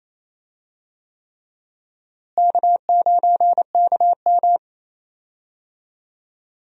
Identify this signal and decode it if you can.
Morse code, heard as a keyed tone, about 28 wpm, 700 Hz. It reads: K9KM
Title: USA Call Signs - 4 Characters - Repeated 3x - 28wpm